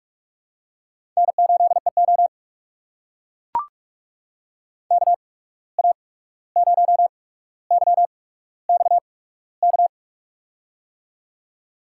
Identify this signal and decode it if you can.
Morse code, heard as a keyed tone, about 44 wpm, spaced as for 17 wpm, 700 Hz. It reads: N8EO KA0YXK
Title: USA Call Signs - 8x Character Spacing - 45wpm